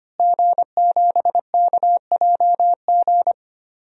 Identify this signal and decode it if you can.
Morse code, heard as a keyed tone, about 25 wpm, 700 Hz. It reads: G7KJG
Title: DXCC-100 Call Signs - 5 Characters - Repeated 3x - 25wpm